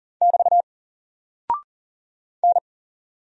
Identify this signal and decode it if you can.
Morse code, heard as a keyed tone, about 39 wpm, 700 Hz. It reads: <BT> N
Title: Single Character (Letter:Number:Punctuation:Prosign) 40wpm